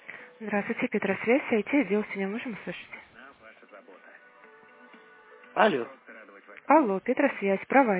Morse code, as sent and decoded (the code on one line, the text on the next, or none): none